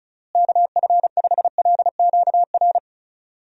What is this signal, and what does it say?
Morse code, heard as a keyed tone, about 34 wpm, 700 Hz. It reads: KF5LQR
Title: USA Call Signs - 6 Characters - Repeated 3x - 35wpm